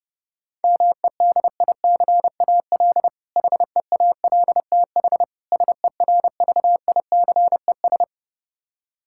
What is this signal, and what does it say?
Morse code, heard as a keyed tone, about 30 wpm, 700 Hz. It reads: MEDICAL HEALTH SERVICES
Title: K1USN SST QSO Practice 30wpm